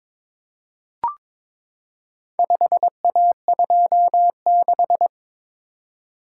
Morse code, heard as a keyed tone, about 22 wpm, 700 Hz. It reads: 5A26
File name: practice / 22wpm / Four Letter-Number Combinations - 22wpm